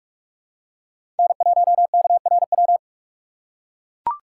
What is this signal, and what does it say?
Morse code, heard as a keyed tone, about 44 wpm, 700 Hz. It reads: N1KRW
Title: USA Call Signs - 5 Characters - 6x Character Spacing - 45wpm